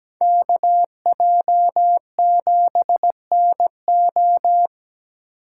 Morse code, heard as a keyed tone, about 20 wpm, 700 Hz. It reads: KJ7NO
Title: Speed Racing - USA 2x2 Call Signs 17wpm